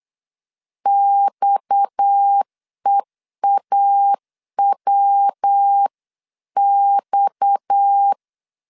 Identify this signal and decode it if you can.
Morse code, heard as a keyed tone, about 8 wpm, 800 Hz. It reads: XEAWX
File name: LEZIONI / 6 PKX0 / 6 PKX 8wpm 3041